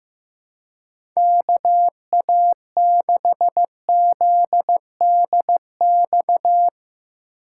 Morse code, heard as a keyed tone, about 16 wpm, 700 Hz. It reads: KA6ZDX
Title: USA Call Signs - 6 Characters - 2x Character Spacing - 15wpm